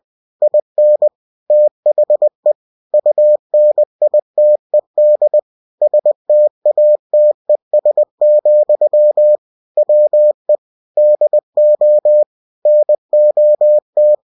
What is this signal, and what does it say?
Morse code, heard as a keyed tone, about 20 wpm, 600 Hz. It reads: IN THE UNITED STATES, WE DO NOT
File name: cw-books / winnie-the-pooh / winnie-20wpm-page-0114